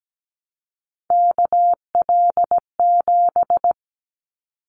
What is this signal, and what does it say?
Morse code, heard as a keyed tone, about 17 wpm, 700 Hz. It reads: KL7
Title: Callsign Prefixes for Top 100 DXCC - 17wpm